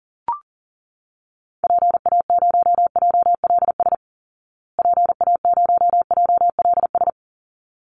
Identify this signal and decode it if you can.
Morse code, heard as a keyed tone, about 40 wpm, 700 Hz. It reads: PA0JLS PA0JLS
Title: Call Signs Repeated 3x - 40wpm